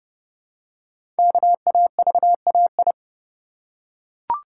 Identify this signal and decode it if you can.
Morse code, heard as a keyed tone, about 30 wpm, 700 Hz. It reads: KAVAI
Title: Code Groups - 5 Letters - 30wpm